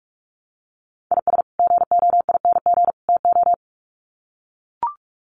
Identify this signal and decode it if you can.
Morse code, heard as a keyed tone, about 44 wpm, 700 Hz. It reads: IS GOING TO
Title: Sets of 3 Words 45wpm